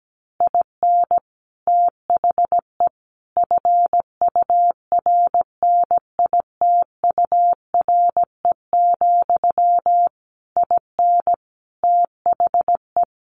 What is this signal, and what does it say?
Morse code, heard as a keyed tone, about 17 wpm, 700 Hz. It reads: IN THE FURNITURE, IN THE
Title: A Sound Of Thunder 17wpm